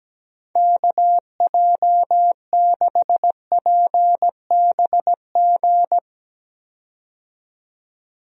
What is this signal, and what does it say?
Morse code, heard as a keyed tone, about 17 wpm, 700 Hz. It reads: KJ6PBG
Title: USA Call Signs - 6 Characters - 17wpm